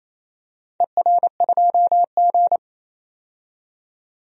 Morse code, heard as a keyed tone, about 28 wpm, 700 Hz. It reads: ER2G
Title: DXCC-100 Call Signs - 4 Characters - 28wpm